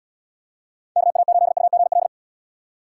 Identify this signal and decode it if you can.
Morse code, heard as a keyed tone, about 77 wpm, 700 Hz. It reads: KA8UGL